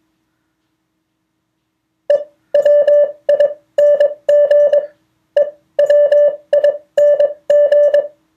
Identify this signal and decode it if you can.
Morse code, heard as a keyed tone, about 22 wpm, spaced as for 17 wpm, 580 Hz. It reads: EWING EWING